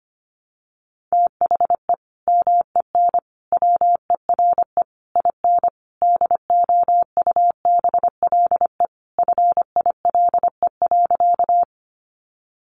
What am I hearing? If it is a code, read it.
Morse code, heard as a keyed tone, about 25 wpm, 700 Hz. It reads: THE MEN WERE IN DOUBLE FILE.
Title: Sentences from Top 1400 Words 25wpm